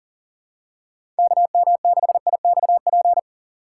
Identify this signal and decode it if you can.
Morse code, heard as a keyed tone, about 40 wpm, 700 Hz. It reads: KM6IXP
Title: USA Call Signs - 6 Characters - 2x Character Spacing - 40wpm